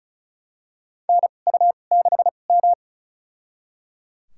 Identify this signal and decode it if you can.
Morse code, heard as a keyed tone, about 34 wpm, spaced as for 25 wpm, 700 Hz. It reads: NU6M
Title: USA Call Signs - 2x Character Spacing - 35wpm